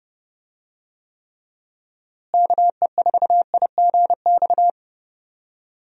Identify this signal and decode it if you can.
Morse code, heard as a keyed tone, about 30 wpm, 700 Hz. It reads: KE4IGX
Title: USA Call Signs - 6 Characters - Repeated 3x - 30wpm